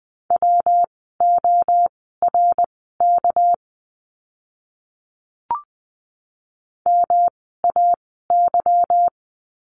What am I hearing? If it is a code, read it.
Morse code, heard as a keyed tone, about 20 wpm, spaced as for 14 wpm, 700 Hz. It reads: WORK MAY
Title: 244 Lesson 39f - Letter z - Words 20wpm